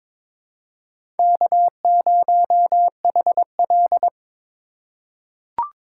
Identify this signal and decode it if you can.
Morse code, heard as a keyed tone, about 22 wpm, 700 Hz. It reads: K0HL
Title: USA Call Signs - 4 Characters - 2x Character Spacing - 22wpm